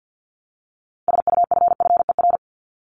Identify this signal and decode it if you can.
Morse code, heard as a keyed tone, about 50 wpm, 700 Hz. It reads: SUFFER
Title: ING Suffix - Top 100 Words with Base Words 50wpm